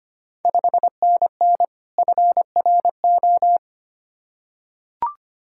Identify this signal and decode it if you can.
Morse code, heard as a keyed tone, about 25 wpm, 700 Hz. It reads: 5NN FRO
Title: Ontario QSO Party Exchange 25wpm